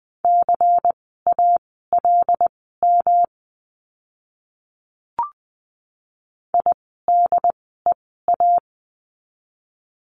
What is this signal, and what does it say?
Morse code, heard as a keyed tone, about 20 wpm, spaced as for 14 wpm, 700 Hz. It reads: CALM IDEA
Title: 108 Lesson 18f - Letter m - Words 20wpm